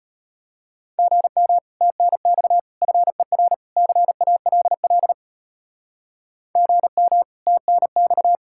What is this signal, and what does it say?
Morse code, heard as a keyed tone, about 33 wpm, 700 Hz. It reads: GM TNX FER CALL GM TNX